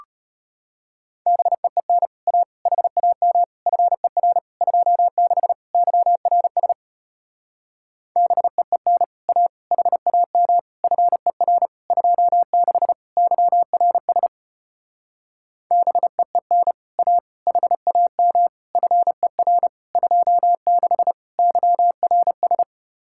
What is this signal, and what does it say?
Morse code, heard as a keyed tone, about 33 wpm, 700 Hz. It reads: BEEN A HAM FER 26 YRS BEEN A HAM FER 26 YRS BEEN A HAM FER 26 YRS